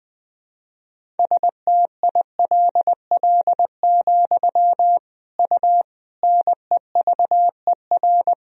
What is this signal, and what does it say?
Morse code, heard as a keyed tone, about 20 wpm, 700 Hz. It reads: STILL, U NEVER
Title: Sentences from Top 300 Words and Abbreviations 20wpm